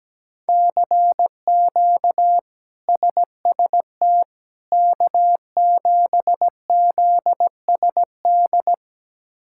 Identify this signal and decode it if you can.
Morse code, heard as a keyed tone, about 17 wpm, 700 Hz. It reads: CQ SST K7ZSD
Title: K1USN SST QSO Practice - 17wpm